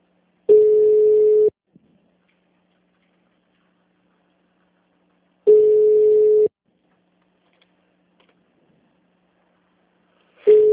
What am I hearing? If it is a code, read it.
Ringback tone (Europe, 425 Hz).